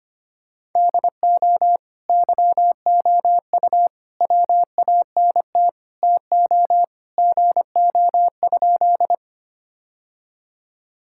Morse code, heard as a keyed tone, about 25 wpm, 700 Hz. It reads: DO YOU WANT TO GO?
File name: practice / 25wpm / Sentences with 2-Word Phrases - 25wpm